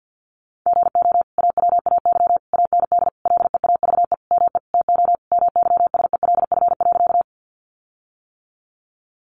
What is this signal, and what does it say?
Morse code, heard as a keyed tone, about 50 wpm, 700 Hz. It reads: GO AWAY AND LEAVE ME TO MYSELF.
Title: Sentences from Top 1300 Words 50wpm